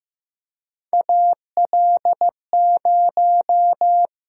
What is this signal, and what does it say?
Morse code, heard as a keyed tone, about 15 wpm, 700 Hz. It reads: AL0
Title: USA Call Signs - 4 Characters - 6x Character Spacing - 15wpm